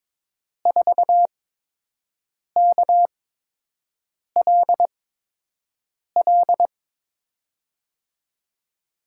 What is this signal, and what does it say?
Morse code, heard as a keyed tone, about 21 wpm, spaced as for 6 wpm, 700 Hz. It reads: 4KLL
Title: USA Call Signs - 8x Character Spacing - 22wpm